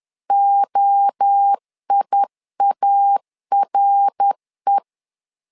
Morse code, heard as a keyed tone, about 10 wpm, 800 Hz. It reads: OIARE
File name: LEZIONI / 4 Q1 10wpm 2448